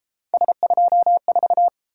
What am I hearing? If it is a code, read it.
Morse code, heard as a keyed tone, about 31 wpm, 700 Hz. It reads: S24